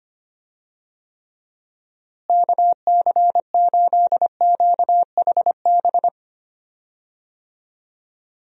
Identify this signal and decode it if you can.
Morse code, heard as a keyed tone, about 25 wpm, 700 Hz. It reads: KC8QHB
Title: USA Call Signs - 6 Characters - Repeated 3x - 25wpm